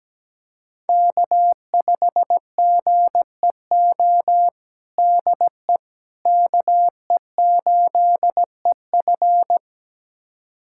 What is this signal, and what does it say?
Morse code, heard as a keyed tone, about 17 wpm, 700 Hz. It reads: K5GEO DE KE8EF
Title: USA Callsign DE Callsign - 5 Characters - 17wpm